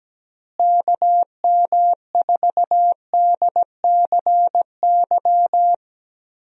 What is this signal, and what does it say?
Morse code, heard as a keyed tone, about 17 wpm, 700 Hz. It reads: KM4DCY
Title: USA Call Signs - 6 Characters - Repeated 2x - 17wpm